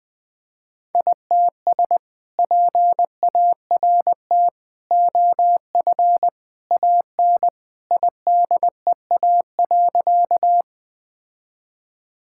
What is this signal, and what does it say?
Morse code, heard as a keyed tone, about 21 wpm, 700 Hz. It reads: ITS PART OF AN IDEA.
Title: Sentences with 2-Word Phrases - 20wpm